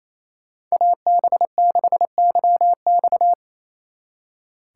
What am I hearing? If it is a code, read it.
Morse code, heard as a keyed tone, about 34 wpm, 700 Hz. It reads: AB6YX
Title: Speed Racing - USA 2x2 Call Signs 28wpm